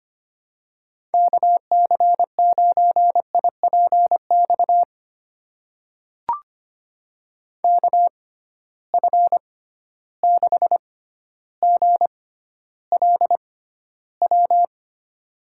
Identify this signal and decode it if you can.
Morse code, heard as a keyed tone, about 25 wpm, spaced as for 13 wpm, 700 Hz. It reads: KC9IPX KF6GLW
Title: USA Call Signs - 6 Characters - 6x Character Spacing - 25wpm